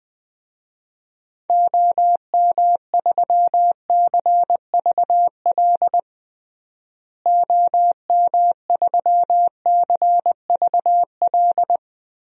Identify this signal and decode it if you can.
Morse code, heard as a keyed tone, about 20 wpm, 700 Hz. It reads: OM3CVL OM3CVL
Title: DXCC-100 Call Signs - 6 Characters - Repeated 3x - 20wpm